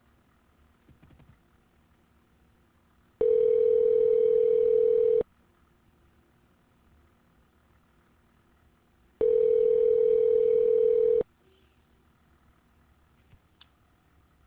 Ringback tone (North America), 440 + 480 Hz.